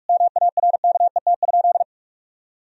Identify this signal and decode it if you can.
Morse code, heard as a keyed tone, about 43 wpm, 700 Hz. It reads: MARKET?
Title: Sentences from Top 1000 Words - Review Part 3 - 45wpm